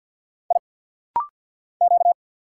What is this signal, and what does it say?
Morse code, heard as a keyed tone, about 50 wpm, 700 Hz. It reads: I Q
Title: Single Letter-Number Alt - Rapid-Fire 50wpm